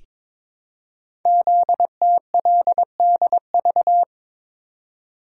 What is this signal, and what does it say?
Morse code, heard as a keyed tone, about 22 wpm, 700 Hz. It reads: ZTLDV
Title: Code Groups - 5 Letters - 22wpm